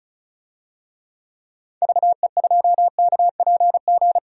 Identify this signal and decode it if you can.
Morse code, heard as a keyed tone, about 36 wpm, 700 Hz. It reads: VE2KPG
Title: DXCC-100 Call Signs - 6 Characters - 35wpm